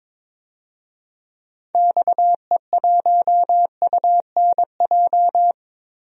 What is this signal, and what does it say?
Morse code, heard as a keyed tone, about 22 wpm, 700 Hz. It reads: XE1UNJ
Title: DXCC-100 Call Signs - 6 Characters - 22wpm